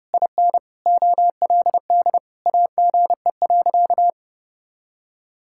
Morse code, heard as a keyed tone, about 30 wpm, 700 Hz. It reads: IN OLD AGE.
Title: Sentences from Top 1000 Words - Review Part 6 - 30wpm